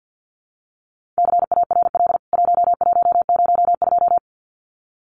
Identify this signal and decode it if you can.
Morse code, heard as a keyed tone, about 49 wpm, 700 Hz. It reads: CARL 1102
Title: CWT Exchange 50wpm